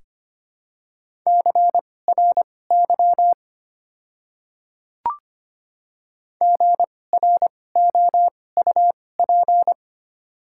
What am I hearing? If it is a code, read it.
Morse code, heard as a keyed tone, about 25 wpm, spaced as for 18 wpm, 690 Hz. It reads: CRY GROUP
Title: 2x Character Spacing - Top 500 Words - 25wpm